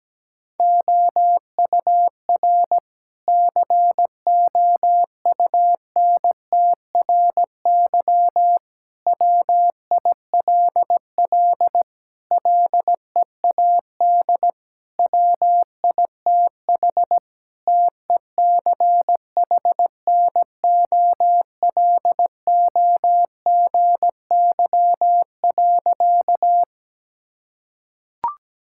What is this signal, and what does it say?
Morse code, heard as a keyed tone, about 17 wpm, 700 Hz. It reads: OUR COUNTRY WILL LEAD WITH TECHNOLOGY.
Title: Sentences from Top 1000 Words - Review Part 5 - 17wpm